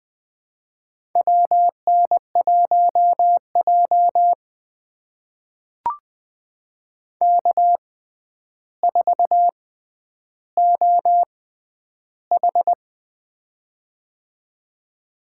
Morse code, heard as a keyed tone, about 20 wpm, spaced as for 10 wpm, 700 Hz. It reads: WN1J K4OH
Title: USA Call Signs - 4 Characters - 6x Character Spacing - 20wpm